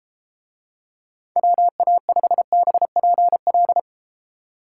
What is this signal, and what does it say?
Morse code, heard as a keyed tone, about 29 wpm, 700 Hz. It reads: WA5BPL